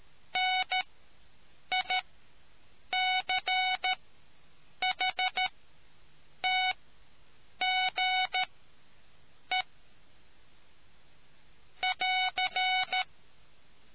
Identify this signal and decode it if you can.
Morse code, heard as a keyed tone, about 13 wpm, spaced as for 7 wpm, 740 Hz. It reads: NICHTGE <AR>